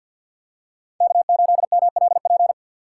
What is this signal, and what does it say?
Morse code, heard as a keyed tone, about 50 wpm, 700 Hz. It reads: K8MLP
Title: USA Call Signs - 5 Characters - 3x Character Spacing - 50wpm